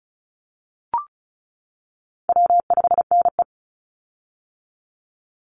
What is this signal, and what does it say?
Morse code, heard as a keyed tone, about 34 wpm, 700 Hz. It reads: W5NE
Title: Call Signs - 35wpm